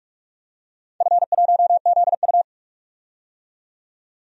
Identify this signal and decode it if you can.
Morse code, heard as a keyed tone, about 44 wpm, 700 Hz. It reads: F1ZU
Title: DXCC-100 Call Signs - 4 Characters - 45wpm